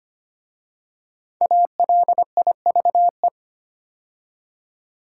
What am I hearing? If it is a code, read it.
Morse code, heard as a keyed tone, about 25 wpm, 700 Hz. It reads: ALIVE
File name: 5 Letter Words - First 3 Letters Given - 25wpm